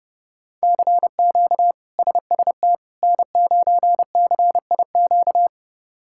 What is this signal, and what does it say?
Morse code, heard as a keyed tone, about 30 wpm, 700 Hz. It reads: CQ SST N9CIQ
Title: K1USN SST QSO Practice - 30wpm